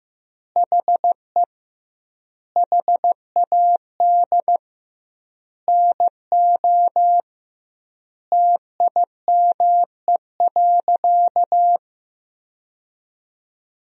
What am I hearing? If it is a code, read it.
Morse code, heard as a keyed tone, about 15 wpm, 700 Hz. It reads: HE HAD NO TIME.
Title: 2x Word-Spacing - Sentences from Top 100 Words - 15wpm